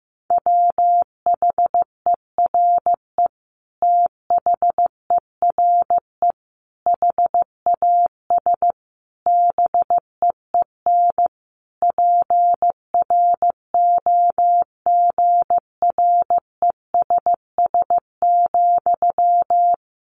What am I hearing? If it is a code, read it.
Morse code, heard as a keyed tone, about 15 wpm, 700 Hz. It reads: WHERE THERE HAS BEEN PROGRESS,
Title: Sentences from Top 1400 Words 15wpm